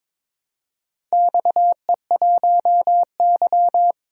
Morse code, heard as a keyed tone, about 22 wpm, 700 Hz. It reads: XE1Y
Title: DXCC-100 Call Signs - 4 Characters - 22wpm